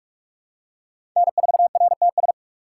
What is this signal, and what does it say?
Morse code, heard as a keyed tone, about 46 wpm, 700 Hz. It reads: N4RTS
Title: USA Call Signs - 3x Character Spacing - 45wpm